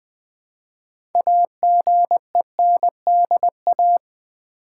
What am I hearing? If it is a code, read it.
Morse code, heard as a keyed tone, about 20 wpm, 700 Hz. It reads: AGENDA
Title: 6 Letter Words - First 3 Letters Given - 20wpm